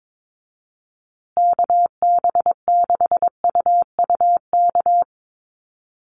Morse code, heard as a keyed tone, about 22 wpm, 700 Hz. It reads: KB6UUK
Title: USA Call Signs Repeated 3x 22wpm